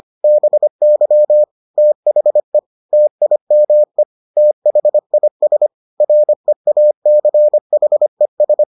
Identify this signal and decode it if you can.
Morse code, heard as a keyed tone, about 25 wpm, 600 Hz. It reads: BY THE TIME THIS REACHES